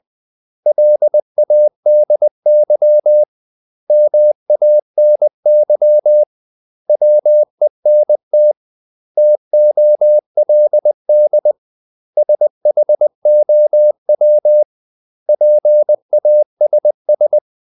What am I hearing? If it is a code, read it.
Morse code, heard as a keyed tone, about 20 wpm, 600 Hz. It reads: LADY MANY WENT TOLD SHOW PASS
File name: CW / Practice / 500MostCommonEnglishWords20